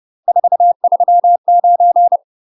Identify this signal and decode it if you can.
Morse code, heard as a keyed tone, about 30 wpm, 700 Hz. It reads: 439
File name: UK MASTER QSOs TRAINING / 30WPM / UK MASTER QSOs 0005